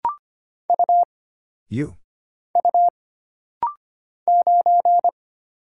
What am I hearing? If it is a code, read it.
Morse code, heard as a keyed tone, about 25 wpm, 700 Hz. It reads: U U 9
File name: Single Character (Letter:Number:Punctuation:Prosign) - Rapid-Fire - 25wpm